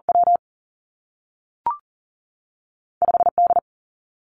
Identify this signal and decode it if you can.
Morse code, heard as a keyed tone, about 40 wpm, 700 Hz. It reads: W 5 D
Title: Two Challenging Letter-Number Combinations 40wpm